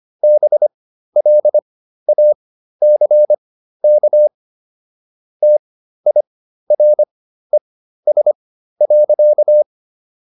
Morse code, heard as a keyed tone, about 25 wpm, spaced as for 13 wpm, 600 Hz. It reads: BLACK TIRES.